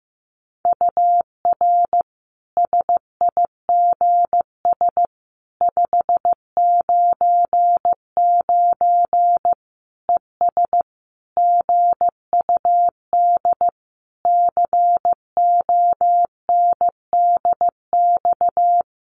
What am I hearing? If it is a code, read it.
Morse code, heard as a keyed tone, about 15 wpm, 700 Hz. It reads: UR SIGS 599 ES GUD CONDX
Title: Common QSO Phrases 15wpm